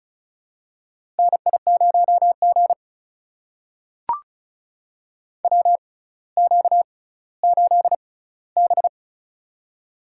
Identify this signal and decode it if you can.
Morse code, heard as a keyed tone, about 36 wpm, spaced as for 18 wpm, 700 Hz. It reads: NI0G WQ8B
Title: USA Call Signs - 4 Characters - 6x Character Spacing - 35wpm